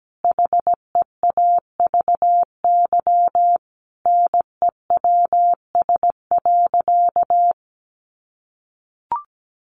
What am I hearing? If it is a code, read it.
Morse code, heard as a keyed tone, about 17 wpm, 700 Hz. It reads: HEAVY NEWS.
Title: Sentences from Top 1100 Words 17wpm